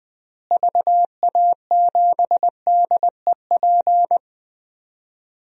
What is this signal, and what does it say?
Morse code, heard as a keyed tone, about 20 wpm, 700 Hz. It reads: VA7DEP
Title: DXCC-100 Call Signs - 6 Characters - Repeated 3x - 20wpm